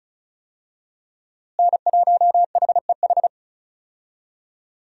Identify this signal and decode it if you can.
Morse code, heard as a keyed tone, about 36 wpm, 700 Hz. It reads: N1HEH
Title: USA Call Signs - 5 Characters - Repeated 2x - 35wpm